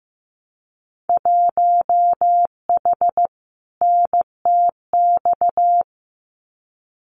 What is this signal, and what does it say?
Morse code, heard as a keyed tone, about 15 wpm, 700 Hz. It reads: 1H NTX
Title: Winter Field Day Exchanges 15wpm